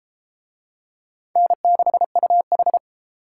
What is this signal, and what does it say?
Morse code, heard as a keyed tone, about 31 wpm, 700 Hz. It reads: N6UH